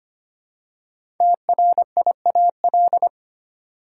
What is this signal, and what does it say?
Morse code, heard as a keyed tone, about 25 wpm, 700 Hz. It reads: TRIAL
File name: 5 Letter Words - First 3 Letters Given - 25wpm